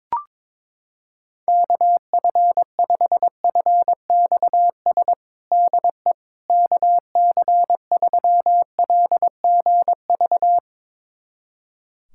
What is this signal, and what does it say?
Morse code, heard as a keyed tone, about 22 wpm, 700 Hz. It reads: KF5FXS DE KC3LGV
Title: USA Callsign DE Callsign - 6 Characters - 22wpm